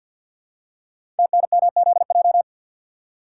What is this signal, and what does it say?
Morse code, heard as a keyed tone, about 52 wpm, 700 Hz. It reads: TNMZJ
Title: Code Groups - 5 Letters - 50wpm